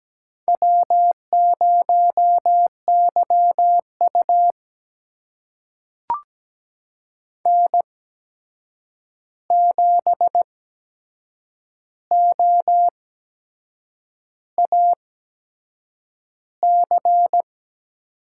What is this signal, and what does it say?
Morse code, heard as a keyed tone, about 17 wpm, spaced as for 7 wpm, 700 Hz. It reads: W0YU N7OAC
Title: USA Call Signs - 8x Character Spacing - 17wpm